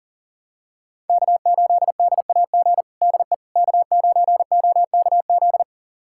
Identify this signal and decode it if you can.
Morse code, heard as a keyed tone, about 37 wpm, 700 Hz. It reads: K8DAG DE K9OKZ
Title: USA Callsign DE Callsign - 5 Characters - 40wpm